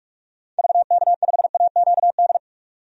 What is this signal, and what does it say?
Morse code, heard as a keyed tone, about 46 wpm, 700 Hz. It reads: VK5AQD
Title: DXCC-100 Call Signs - 6 Characters - 45wpm